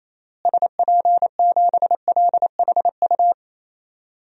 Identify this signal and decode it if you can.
Morse code, heard as a keyed tone, about 28 wpm, 700 Hz. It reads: SP7LHU
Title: DXCC-100 Call Signs - 6 Characters - Repeated 3x - 28wpm